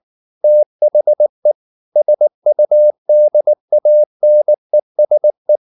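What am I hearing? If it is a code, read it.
Morse code, heard as a keyed tone, about 19 wpm, 600 Hz. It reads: THE SUDANESE
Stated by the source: Morse code at 19 wpm from CW med Gnister, reading Ken Follett's novel Never in English.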